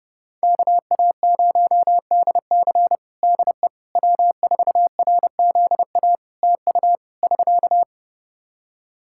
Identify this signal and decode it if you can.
Morse code, heard as a keyed tone, about 32 wpm, 700 Hz. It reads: KA0DC DE W4RZA TU <SK>
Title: Practice QSO - Exchange 4 - Ending - 30wpm